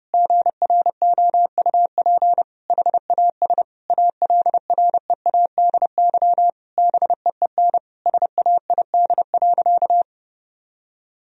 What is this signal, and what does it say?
Morse code, heard as a keyed tone, about 32 wpm, 700 Hz. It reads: GROUP HAS ALREADY BEEN SAID.
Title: Sentences from Top 1000 Words - Review Part 1 - 30wpm